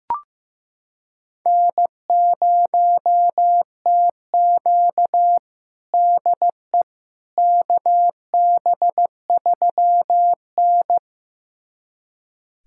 Morse code, heard as a keyed tone, about 15 wpm, 700 Hz. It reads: N0TQ DE KB3N
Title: USA Callsign DE Callsign - 4 Characters - 15wpm